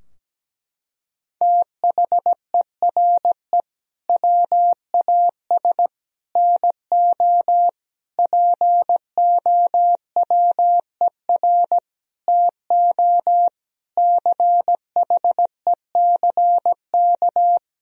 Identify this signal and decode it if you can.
Morse code, heard as a keyed tone, about 17 wpm, 700 Hz. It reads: THERE WAS NO POWER TO CHECK
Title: Sentences from Top 1000 Words - Review Part 3 - 17wpm